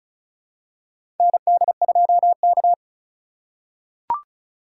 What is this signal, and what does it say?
Morse code, heard as a keyed tone, about 36 wpm, 700 Hz. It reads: ND2K
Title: USA Call Signs - 4 Characters - 2x Character Spacing - 35wpm